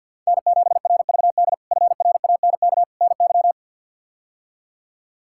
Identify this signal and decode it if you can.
Morse code, heard as a keyed tone, about 52 wpm, 700 Hz. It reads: N7RVD FRANK NY
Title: K1USN SST QSO Practice - 50wpm